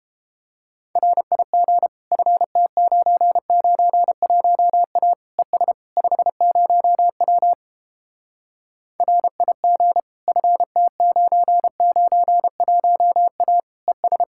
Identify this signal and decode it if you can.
Morse code, heard as a keyed tone, about 29 wpm, 700 Hz. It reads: RIG FT991A ES 50W RIG FT991A ES